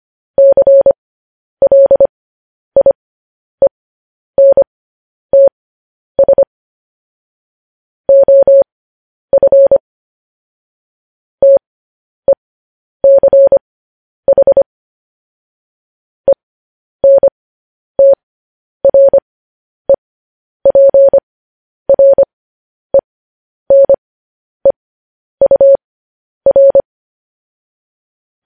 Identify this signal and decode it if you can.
Morse code, heard as a keyed tone, about 25 wpm, spaced as for 10 wpm, 550 Hz. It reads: CLIENTS OF TECH ENTREPRENEUR
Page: news headlines in morse code at 10 wpm